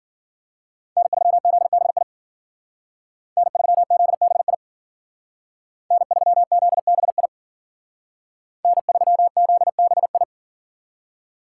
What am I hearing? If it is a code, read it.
Morse code, heard as a keyed tone, about 50 wpm, 700 Hz. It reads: N3ZBI N3ZBI N3ZBI N3ZBI